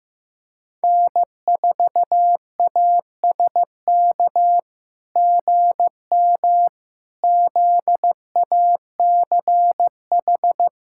Morse code, heard as a keyed tone, about 16 wpm, 700 Hz. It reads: N4ASK GM ZACH
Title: SOTA QSO Practice 15wpm